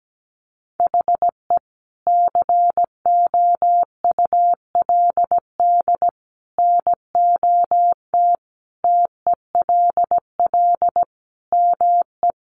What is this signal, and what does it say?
Morse code, heard as a keyed tone, about 17 wpm, 700 Hz. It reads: HE COULD NOT TELL ME
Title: Sentences from Top 600 Words 17wpm